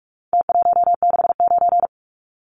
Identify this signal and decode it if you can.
Morse code, heard as a keyed tone, about 44 wpm, 700 Hz. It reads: T169
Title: Numbers - Contest Style - 4-Digit 45wpm